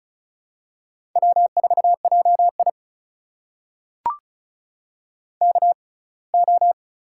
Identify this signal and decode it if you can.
Morse code, heard as a keyed tone, about 36 wpm, spaced as for 24 wpm, 700 Hz. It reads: W4JI KO
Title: USA Call Signs - 4 Characters - 6x Character Spacing - 35wpm